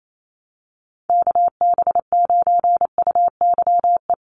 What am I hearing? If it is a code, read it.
Morse code, heard as a keyed tone, about 28 wpm, 700 Hz. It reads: KB9UYE
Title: USA Call Signs Repeated 3x 28wpm